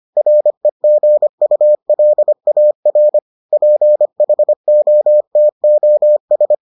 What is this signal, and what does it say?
Morse code, heard as a keyed tone, about 25 wpm, 600 Hz. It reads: REGULAR PHOTOS